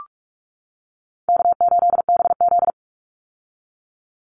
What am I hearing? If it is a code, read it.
Morse code, heard as a keyed tone, about 44 wpm, 700 Hz. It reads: K8BZ
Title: Easy Call Signs - 45wpm